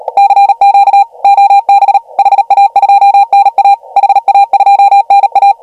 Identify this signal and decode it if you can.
Morse code, heard as a keyed tone, about 38 wpm, 2400 Hz, with noise. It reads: CQ OB HA2NA HA2NA
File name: AUDIO2019 / HG1G CW